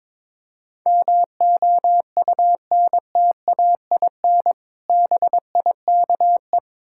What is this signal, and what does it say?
Morse code, heard as a keyed tone, about 22 wpm, 690 Hz. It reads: MOUNTAIN BIKE
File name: Difficult Words with Hints - 22wpm